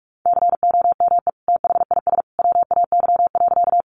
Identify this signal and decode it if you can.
Morse code, heard as a keyed tone, about 46 wpm, 700 Hz. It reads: COME THIS WAY.
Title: Sentences from Top 100 Words 45wpm